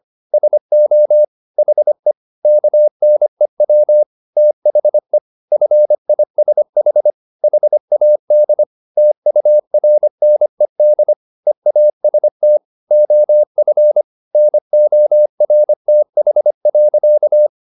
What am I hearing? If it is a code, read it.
Morse code, heard as a keyed tone, about 25 wpm, 600 Hz. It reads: SO HE KNEW THE FISH HAD TURNED EAST OF NORTH.
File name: the old man and the sea part 1 25wpm